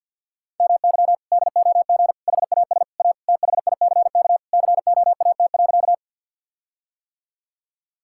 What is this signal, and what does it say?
Morse code, heard as a keyed tone, about 50 wpm, 700 Hz. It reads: MY DOG HAS A THICK COAT.
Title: Sentences from Top 1600 Words 50wpm